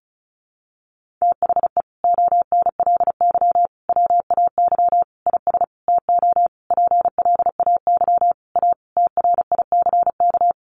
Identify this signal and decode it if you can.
Morse code, heard as a keyed tone, about 34 wpm, 700 Hz. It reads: THE ONLY WAY IS TO PLAY A TRICK
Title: Sentences from Top 1500 Words 35wpm